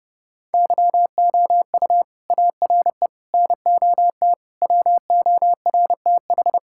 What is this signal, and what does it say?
Morse code, heard as a keyed tone, about 28 wpm, 700 Hz. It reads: YOU ARE NOT WORTH
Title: Sentences from Top 1600 Words 30wpm